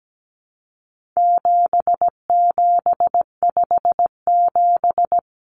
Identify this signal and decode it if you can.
Morse code, heard as a keyed tone, about 17 wpm, 700 Hz. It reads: 7757
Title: Numbers - 4-Digit 17wpm